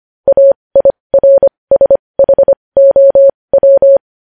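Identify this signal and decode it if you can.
Morse code, heard as a keyed tone, about 25 wpm, spaced as for 20 wpm, 550 Hz. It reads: AIRSHOW
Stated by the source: Morse code news headlines sent at 20.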